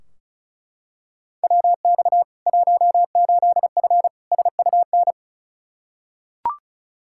Morse code, heard as a keyed tone, about 34 wpm, 700 Hz. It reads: WX 18F SUN
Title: Practice QSO - Exchange 2 - Info - 35wpm